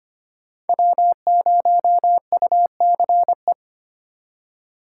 Morse code, heard as a keyed tone, about 25 wpm, 700 Hz. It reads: W0UCE
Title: K1USN SST QSO Practice - 25wpm